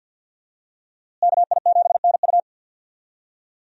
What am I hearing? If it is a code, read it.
Morse code, heard as a keyed tone, about 52 wpm, 700 Hz. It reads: KI7NU
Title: USA Call Signs - 5 Characters - Repeated 3x - 50wpm